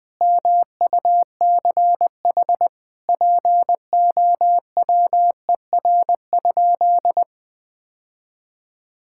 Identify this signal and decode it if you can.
Morse code, heard as a keyed tone, about 21 wpm, 700 Hz. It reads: MUCH POWER?
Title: Sentences from Top 1000 Words - Review Part 5 - 20wpm